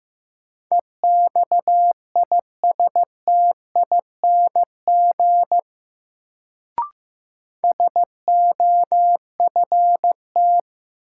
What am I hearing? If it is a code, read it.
Morse code, heard as a keyed tone, about 15 wpm, 700 Hz. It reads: EXISTING SOFT
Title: Sentences from Top 1500 Words - New Word Drill - 15wpm